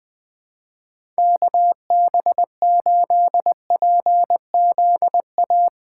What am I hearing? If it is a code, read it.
Morse code, heard as a keyed tone, about 20 wpm, 700 Hz. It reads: KB8PZA
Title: USA Call Signs - 6 Characters - 3x Character Spacing - 20wpm